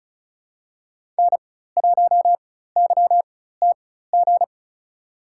Morse code, heard as a keyed tone, about 34 wpm, spaced as for 16 wpm, 700 Hz. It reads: N1YTG
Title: USA Call Signs - 5 Characters - 4x Character Spacing - 35wpm